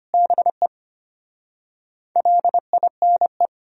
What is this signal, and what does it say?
Morse code, heard as a keyed tone, about 33 wpm, 700 Hz. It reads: BE LINE